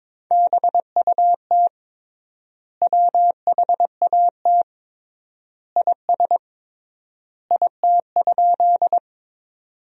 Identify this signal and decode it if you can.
Morse code, heard as a keyed tone, about 22 wpm, 700 Hz. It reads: BUT WHAT IS IT?
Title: 3x Word-Spacing - Sentences from Top 100 Words - 22wpm